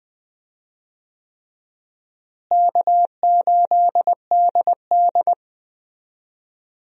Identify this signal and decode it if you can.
Morse code, heard as a keyed tone, about 21 wpm, 700 Hz. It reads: K8DD
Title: USA Call Signs - 4 Characters - Repeated 3x - 20wpm